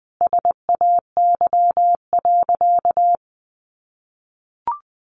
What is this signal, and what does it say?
Morse code, heard as a keyed tone, about 20 wpm, 700 Hz. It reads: SAY.
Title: Sentences from Top 1200 Words 20wpm